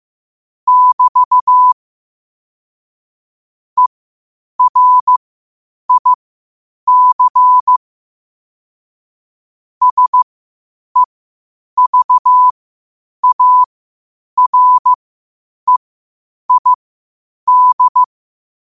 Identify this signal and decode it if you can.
Morse code, heard as a keyed tone, about 15 wpm, spaced as for 8 wpm, 1000 Hz. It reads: <BT> ERIC SEVAREID
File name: qotdfiles / 8wpm